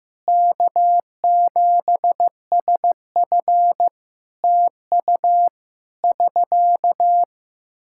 Morse code, heard as a keyed tone, about 16 wpm, 700 Hz. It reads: K7SF TU <SK>
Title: Practice QSO - Exchange 4 - Ending - 15wpm